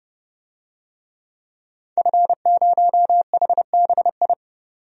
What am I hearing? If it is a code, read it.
Morse code, heard as a keyed tone, about 31 wpm, 700 Hz. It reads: F0HBI
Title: DXCC-100 Call Signs - 5 Characters - 30wpm